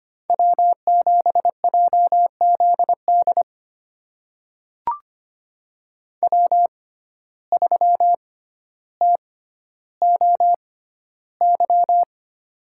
Morse code, heard as a keyed tone, about 25 wpm, spaced as for 13 wpm, 700 Hz. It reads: W7JZD W3TOY
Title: USA Call Signs - 5 Characters - 6x Character Spacing - 25wpm